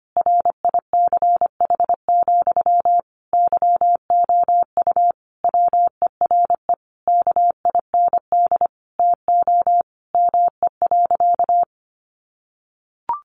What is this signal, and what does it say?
Morse code, heard as a keyed tone, about 25 wpm, 700 Hz. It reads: RICH, YOU WERE KIND TO ME.
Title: Sentences from Top 1100 Words 25wpm